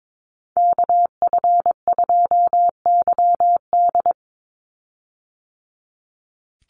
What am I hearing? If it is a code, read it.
Morse code, heard as a keyed tone, about 22 wpm, 700 Hz. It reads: KF2YD
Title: USA Call Signs 22wpm